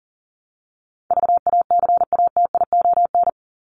Morse code, heard as a keyed tone, about 40 wpm, 700 Hz. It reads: VACATION
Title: Top 1501-1750 Words 40wpm